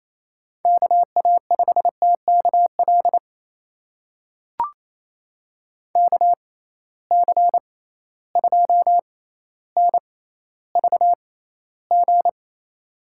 Morse code, heard as a keyed tone, about 28 wpm, spaced as for 14 wpm, 700 Hz. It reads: KA5TKL KC2NVG
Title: USA Call Signs - 6 Characters - 6x Character Spacing - 28wpm